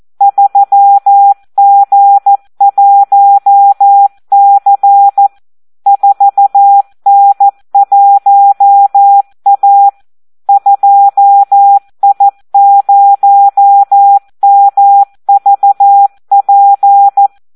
Morse code, heard as a keyed tone, about 14 wpm, 800 Hz. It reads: 3G1C 4N1A 2I0MVP
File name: Audio practica CW / SD 1 14